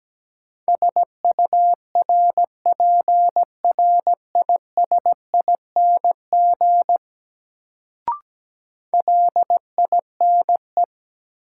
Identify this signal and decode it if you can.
Morse code, heard as a keyed tone, about 17 wpm, 700 Hz. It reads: SURPRISING LINE
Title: Sets of 5 Words - New Word Drill - 17wpm